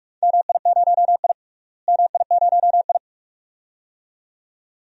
Morse code, heard as a keyed tone, about 46 wpm, 700 Hz. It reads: MI0I MI0I
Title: DXCC-100 Call Signs - 4 Characters - Repeated 2x - 45wpm